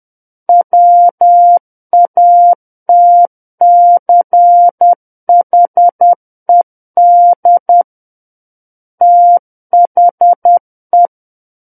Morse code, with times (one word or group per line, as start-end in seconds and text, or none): 0.5-7.8 WATCHED
9.0-11.1 THE